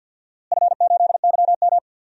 Morse code, heard as a keyed tone, about 52 wpm, 700 Hz. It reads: F8YM
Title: DXCC-100 Call Signs - 4 Characters - Repeated 3x - 50wpm